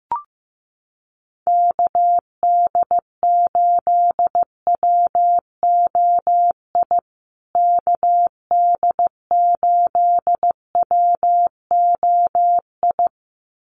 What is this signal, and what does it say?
Morse code, heard as a keyed tone, about 15 wpm, 700 Hz. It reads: KD8WOI KD8WOI
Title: USA Call Signs Repeated 2x 15wpm